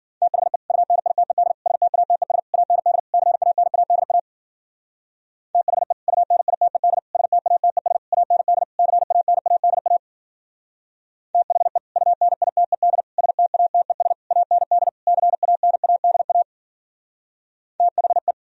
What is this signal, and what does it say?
Morse code, heard as a keyed tone, about 50 wpm, 700 Hz. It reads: THE UNITED STATES AND CANADA THE UNITED STATES AND CANADA THE UNITED STATES AND CANADA THE
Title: Speed-Racing - Sets of 5 Words 40wpm